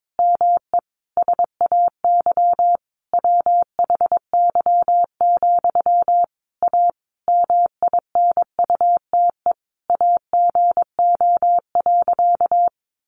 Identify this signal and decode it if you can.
Morse code, heard as a keyed tone, about 22 wpm, 700 Hz. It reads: ME SAY WHY, A MINUTE AGO.
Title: Sentences from Top 500 Words 22wpm